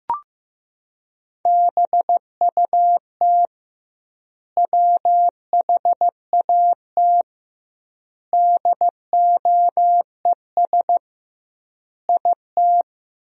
Morse code, heard as a keyed tone, about 15 wpm, 700 Hz. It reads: BUT WHAT DOES IT
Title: 2x Word-Spacing - Sentences from Top 200 Words - 15wpm